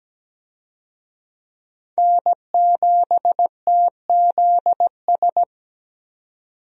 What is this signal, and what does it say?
Morse code, heard as a keyed tone, about 17 wpm, 700 Hz. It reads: N7TZS
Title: USA Call Signs - 5 Characters - 17wpm